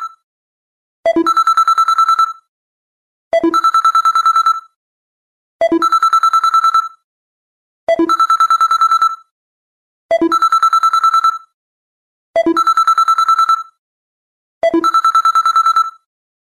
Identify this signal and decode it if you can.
There is no code.